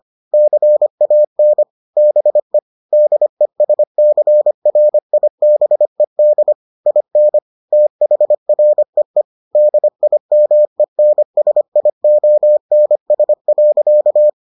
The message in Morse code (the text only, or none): CAN BE DESCRIBED IN THREE DIMENSIONS.